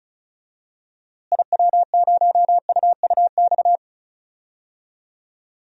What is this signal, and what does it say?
Morse code, heard as a keyed tone, about 36 wpm, 700 Hz. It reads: IW0UUX
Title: DXCC-100 Call Signs - 6 Characters - Repeated 2x - 35wpm